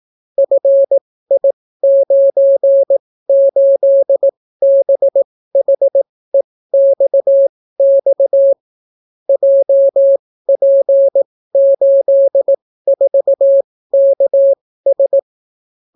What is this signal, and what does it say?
Morse code, heard as a keyed tone, about 18 wpm, spaced as for 14 wpm, 550 Hz. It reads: FI98BHEXX JP84KS